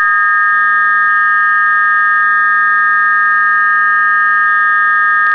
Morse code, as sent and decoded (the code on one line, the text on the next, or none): none